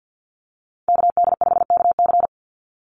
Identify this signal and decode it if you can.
Morse code, heard as a keyed tone, about 50 wpm, 700 Hz. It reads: KD5KC
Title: NA SOTA Activator Callsigns 50wpm